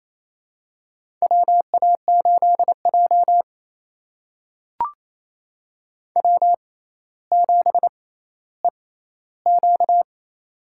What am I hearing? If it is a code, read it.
Morse code, heard as a keyed tone, about 28 wpm, spaced as for 14 wpm, 700 Hz. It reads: WA8J W7EQ
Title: USA Call Signs - 4 Characters - 6x Character Spacing - 28wpm